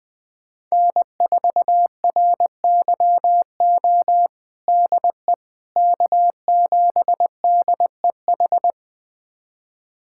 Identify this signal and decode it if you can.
Morse code, heard as a keyed tone, about 20 wpm, 700 Hz. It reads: N4RYO DE K7DEH
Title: USA Callsign DE Callsign - 5 Characters - 20wpm